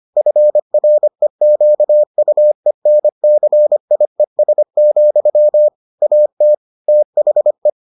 Morse code, heard as a keyed tone, about 25 wpm, 600 Hz. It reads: FREQUENCIES, AT THE